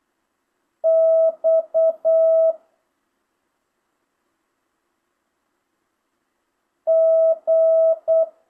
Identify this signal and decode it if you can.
Morse code, heard as a keyed tone, about 8 wpm, 650 Hz. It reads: X G
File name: jh8cbh / 2025cw